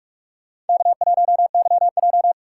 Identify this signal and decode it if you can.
Morse code, heard as a keyed tone, about 43 wpm, 700 Hz. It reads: K1YJ
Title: USA Call Signs - 2x Character Spacing - 45wpm